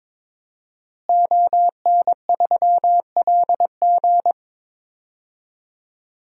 Morse code, heard as a keyed tone, about 22 wpm, 700 Hz. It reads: ON3LG
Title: DXCC-100 Call Signs - 5 Characters - Repeated 2x - 22wpm